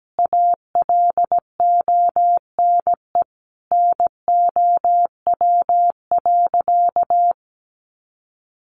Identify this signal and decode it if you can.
Morse code, heard as a keyed tone, about 17 wpm, 700 Hz. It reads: ALONE NOW.
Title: Sentences from Top 1100 Words 17wpm